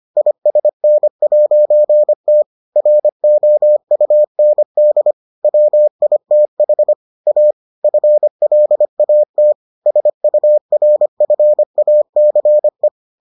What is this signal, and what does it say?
Morse code, heard as a keyed tone, about 25 wpm, 600 Hz. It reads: ISN'T ROUND WITH A FLAT SURFACE